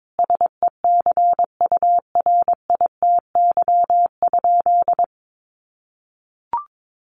Morse code, heard as a keyed tone, about 22 wpm, 700 Hz. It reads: SECURITY?
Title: Sentences from Top 1400 Words 22wpm